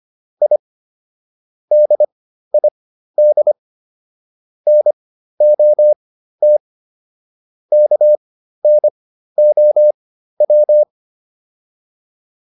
Morse code, heard as a keyed tone, about 25 wpm, spaced as for 13 wpm, 600 Hz. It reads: I DID NOT KNOW